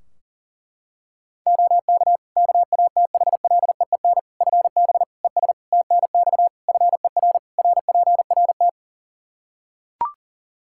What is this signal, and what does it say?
Morse code, heard as a keyed tone, about 40 wpm, 700 Hz. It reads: OK KATHLEEN FB ES TNX FER RPRT
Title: Practice QSO - Exchange 2 - Info - 40wpm